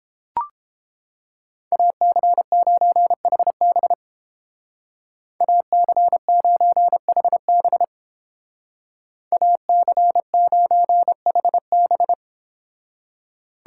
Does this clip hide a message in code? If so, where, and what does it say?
Yes: Morse code from 1.7 to 12.1 s, heard as a keyed tone, about 30 wpm, 700 Hz. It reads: AC9HB AC9HB AC9HB